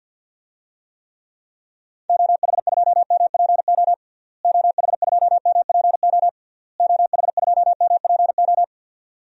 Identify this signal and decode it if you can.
Morse code, heard as a keyed tone, about 51 wpm, 700 Hz. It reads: OH2MPO OH2MPO OH2MPO
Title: DXCC-100 Call Signs - 6 Characters - Repeated 3x - 50wpm